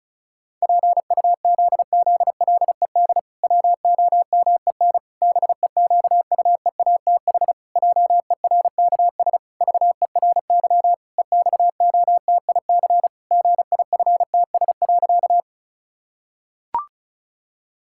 Morse code, heard as a keyed tone, about 34 wpm, 700 Hz. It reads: PUZZLED WOMEN BEQUEATH JERKS VERY EXOTIC GIFTS.
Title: Pangram Challenge v2 - 35wpm